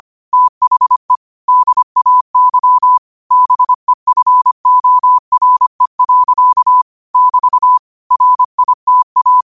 Morse code, heard as a keyed tone, about 25 wpm, 1000 Hz. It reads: THE DAY BEFORE. <BT> RITA
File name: 25wpm